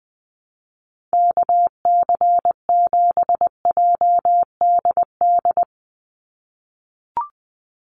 Morse code, heard as a keyed tone, about 20 wpm, 700 Hz. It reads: KC7JDD
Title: USA Call Signs 20wpm